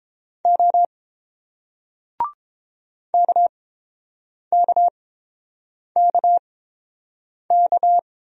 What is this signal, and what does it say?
Morse code, heard as a keyed tone, about 30 wpm, spaced as for 7 wpm, 700 Hz. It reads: O KKKK